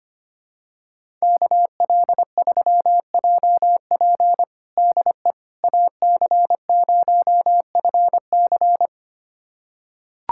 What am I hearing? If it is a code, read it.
Morse code, heard as a keyed tone, about 25 wpm, 700 Hz. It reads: KL3JP DE AC0FC
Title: USA Callsign DE Callsign - 5 Characters - 25wpm